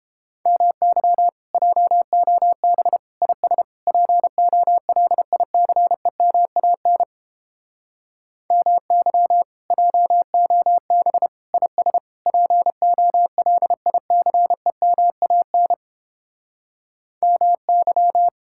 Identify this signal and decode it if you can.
Morse code, heard as a keyed tone, about 29 wpm, 700 Hz. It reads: MY JOB IS POLICEMAN MY JOB IS POLICEMAN MY